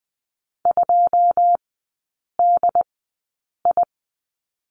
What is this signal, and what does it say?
Morse code, heard as a keyed tone, about 20 wpm, spaced as for 8 wpm, 700 Hz. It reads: 2DI
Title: 093 Lesson 16e - Letter c - Three Characters 20wpm